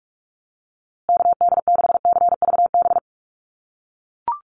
Morse code, heard as a keyed tone, about 44 wpm, 700 Hz. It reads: KD6CVB
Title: USA Call Signs Repeated 3x 45wpm